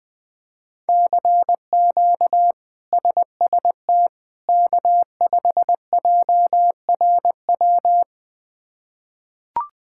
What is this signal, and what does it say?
Morse code, heard as a keyed tone, about 20 wpm, 700 Hz. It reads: CQ SST K5JRW
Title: K1USN SST QSO Practice - 20wpm